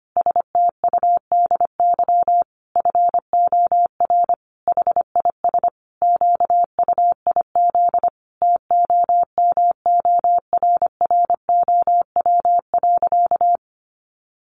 Morse code, heard as a keyed tone, about 25 wpm, 700 Hz. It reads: STUDY FOR HIS QUIZ TOMORROW.
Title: Additional Easy Phrases 25wpm